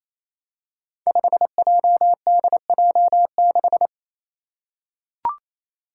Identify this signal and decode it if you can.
Morse code, heard as a keyed tone, about 28 wpm, 700 Hz. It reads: 5JDJ6
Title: Five Challenging Letter-Number Combinations 28wpm